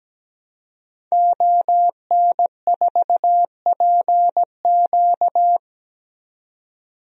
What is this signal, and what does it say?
Morse code, heard as a keyed tone, about 17 wpm, 700 Hz. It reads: ON4PQ
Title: DXCC-100 Call Signs - 5 Characters - 17wpm